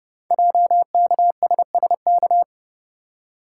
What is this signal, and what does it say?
Morse code, heard as a keyed tone, about 30 wpm, 700 Hz. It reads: JKSSK
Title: Code Groups - 5 Letter:Numbers - 30wpm